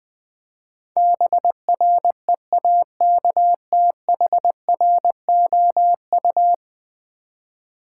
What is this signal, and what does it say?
Morse code, heard as a keyed tone, about 20 wpm, 700 Hz. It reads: BREAKTHROU
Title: Build A Word - 5 Letters At A Time - 12-Letter Words - 20wpm